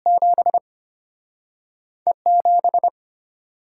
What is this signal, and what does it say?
Morse code, heard as a keyed tone, about 34 wpm, spaced as for 6 wpm, 690 Hz. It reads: 7E 7